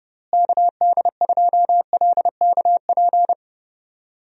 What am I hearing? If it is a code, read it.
Morse code, heard as a keyed tone, about 30 wpm, 700 Hz. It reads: KD2LKP
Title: USA Call Signs - 6 Characters - Repeated 3x - 30wpm